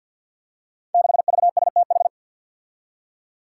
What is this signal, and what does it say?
Morse code, heard as a keyed tone, about 53 wpm, 700 Hz. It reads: 6VSTH
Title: Five Challenging Letter-Number Combinations 50wpm